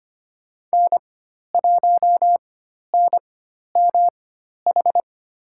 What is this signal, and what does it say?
Morse code, heard as a keyed tone, about 25 wpm, spaced as for 12 wpm, 700 Hz. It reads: N1NMH
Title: USA Call Signs - 5 Characters - 4x Character Spacing - 25wpm